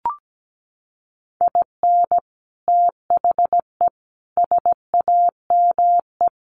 Morse code, heard as a keyed tone, about 17 wpm, 700 Hz. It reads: IN THE SAME
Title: Sets of 4 Words 17wpm